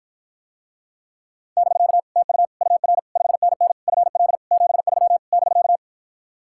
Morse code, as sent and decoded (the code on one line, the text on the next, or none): -...-.- / - ..- / ..- .-. / ..... -. -. / ..-. .-.. / --... ...-- / -...-.-
<BK> TU UR 5NN FL 73 <BK>